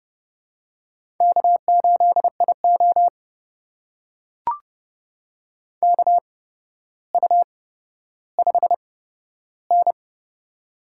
Morse code, heard as a keyed tone, about 30 wpm, spaced as for 13 wpm, 700 Hz. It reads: K8IO KU5N
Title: USA Call Signs - 4 Characters - 8x Character Spacing - 30wpm